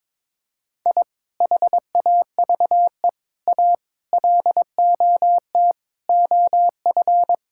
Morse code, heard as a keyed tone, about 22 wpm, 700 Hz. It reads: I HAVE A LOT OF
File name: practice / 22wpm / Build A Phrase - 5-Word Phrases - 22wpm